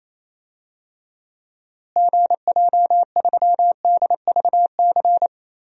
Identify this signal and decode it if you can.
Morse code, heard as a keyed tone, about 28 wpm, 700 Hz. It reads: GJ3DVC
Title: DXCC-100 Call Signs - 6 Characters - Repeated 2x - 28wpm